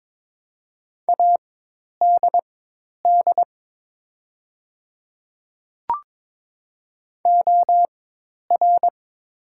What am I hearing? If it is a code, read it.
Morse code, heard as a keyed tone, about 22 wpm, spaced as for 10 wpm, 700 Hz. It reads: ADD OR
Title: 4x Character Spacing - Top 500 Words - 22wpm